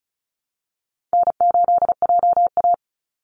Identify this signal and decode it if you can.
Morse code, heard as a keyed tone, about 34 wpm, 700 Hz. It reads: N8JA
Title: Easy Call Signs Repeated 2x - 35wpm